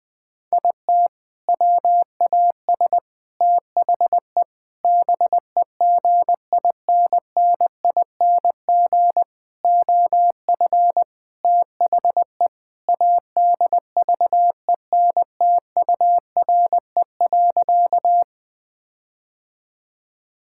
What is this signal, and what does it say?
Morse code, heard as a keyed tone, about 20 wpm, 700 Hz. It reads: IT WAS THE BEGINNING OF THE ADVENTURE.
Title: Sentences with 4-Word Phrases - 20wpm